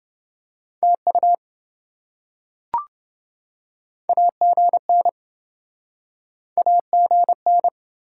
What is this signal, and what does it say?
Morse code, heard as a keyed tone, about 26 wpm, 700 Hz. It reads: TU AGN AGN